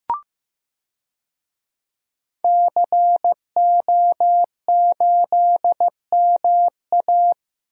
Morse code, heard as a keyed tone, about 15 wpm, 700 Hz. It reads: CO8MA
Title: DXCC-100 Call Signs - 5 Characters - Repeated 2x - 15wpm